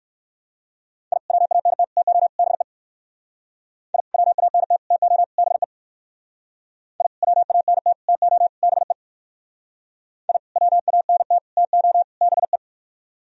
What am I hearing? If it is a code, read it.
Morse code, heard as a keyed tone, about 58 wpm, 700 Hz. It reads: I WANT TO BE I WANT TO BE I WANT TO BE I WANT TO BE